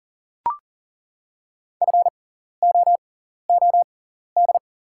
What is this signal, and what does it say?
Morse code, heard as a keyed tone, about 40 wpm, spaced as for 14 wpm, 700 Hz. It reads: FOOD
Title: 6x Character Spacing - Top 300 Words - 40wpm